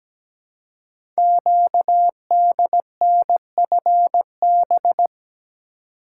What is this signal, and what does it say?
Morse code, heard as a keyed tone, about 17 wpm, 700 Hz. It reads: QDNFB
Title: Code Groups - 5 Letters - 17wpm